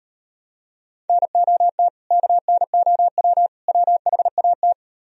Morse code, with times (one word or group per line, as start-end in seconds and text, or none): 1.1-1.9 NOT
2.1-3.5 KNOW
3.7-4.7 WHAT